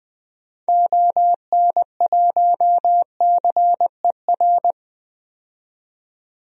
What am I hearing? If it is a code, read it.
Morse code, heard as a keyed tone, about 20 wpm, 700 Hz. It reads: ON1CER
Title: DXCC-100 Call Signs - 6 Characters - 20wpm